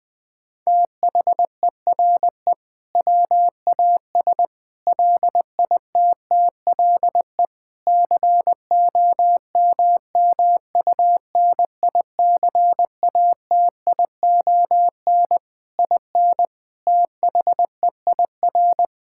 Morse code, heard as a keyed tone, about 20 wpm, 700 Hz. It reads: THERE WAS LITTLE COMMUNICATION IN THEIR
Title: Sentences from Top 1000 Words - Review Part 6 - 20wpm